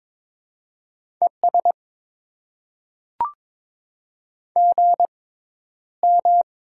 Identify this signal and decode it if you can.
Morse code, heard as a keyed tone, about 22 wpm, spaced as for 11 wpm, 700 Hz. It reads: ES GM
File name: Top 20 QSO Elements - 6x Character Spacing - 22wpm